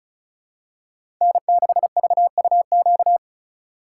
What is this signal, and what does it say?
Morse code, heard as a keyed tone, about 34 wpm, 700 Hz. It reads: N6VUQ
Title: USA Call Signs - 5 Characters - 2x Character Spacing - 35wpm